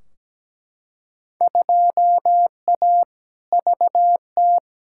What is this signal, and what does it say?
Morse code, heard as a keyed tone, about 17 wpm, 700 Hz. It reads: 2A VT
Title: Field Day Exchanges Encore - 17wpm